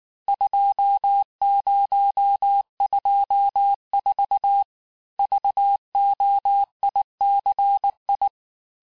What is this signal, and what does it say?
Morse code, heard as a keyed tone, about 19 wpm, 800 Hz. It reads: 2024 VOICI
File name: cours-du-lundi files / cours-1583--texte-chiffre-18-mpm